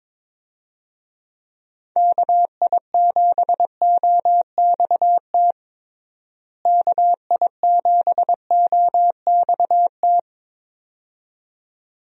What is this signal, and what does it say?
Morse code, heard as a keyed tone, about 21 wpm, 700 Hz. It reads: KI7OXT KI7OXT
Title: USA Call Signs - 6 Characters - Repeated 2x - 22wpm